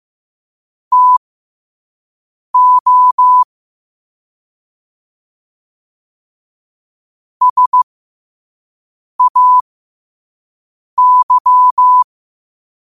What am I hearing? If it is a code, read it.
Morse code, heard as a keyed tone, about 15 wpm, spaced as for 5 wpm, 1000 Hz. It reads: TO SAY
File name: qotdfiles / 5wpm